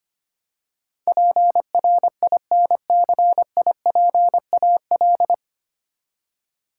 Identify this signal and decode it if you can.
Morse code, heard as a keyed tone, about 25 wpm, 700 Hz. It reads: PRINCIPAL
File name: Nine Letter Words - 25wpm